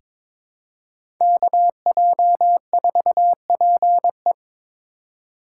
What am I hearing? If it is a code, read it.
Morse code, heard as a keyed tone, about 22 wpm, 700 Hz. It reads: KJ4PE
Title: USA Call Signs - 4x Character Spacing - 22wpm